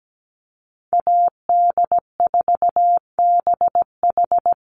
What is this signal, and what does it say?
Morse code, heard as a keyed tone, about 17 wpm, 700 Hz. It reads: AD4BH
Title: USA Call Signs Repeated 2x 17wpm